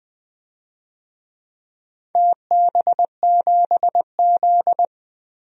Morse code, heard as a keyed tone, about 20 wpm, 700 Hz. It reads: TB7Z
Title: DXCC-100 Call Signs - 4 Characters - 20wpm